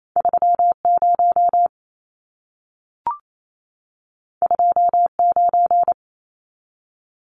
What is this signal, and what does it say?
Morse code, heard as a keyed tone, about 28 wpm, 700 Hz. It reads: 30 29
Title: Numbers - 2-Digit 28wpm